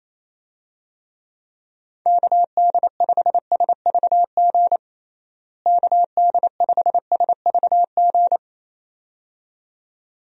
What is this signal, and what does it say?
Morse code, heard as a keyed tone, about 28 wpm, 700 Hz. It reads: KD5SVG KD5SVG
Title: USA Call Signs - 6 Characters - Repeated 2x - 28wpm